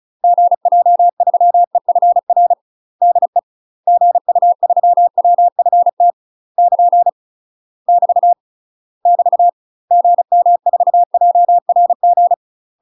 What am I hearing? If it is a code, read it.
Morse code, heard as a keyed tone, about 35 wpm, 700 Hz. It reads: GJ3EFR DE GU3WFT <KN> <BT> <BT> GM4JRG
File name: UK MASTER QSOs TRAINING / 35WPM / UK MASTER QSOs 0018